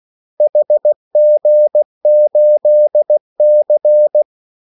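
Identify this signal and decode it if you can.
Morse code, heard as a keyed tone, about 16 wpm, 600 Hz. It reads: HG8C
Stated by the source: Morse call sign practice at 16 wpm.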